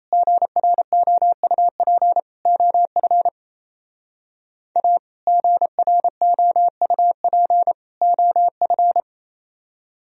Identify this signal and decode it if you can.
Morse code, heard as a keyed tone, about 37 wpm, 700 Hz. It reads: GROUP OF A GROUP OF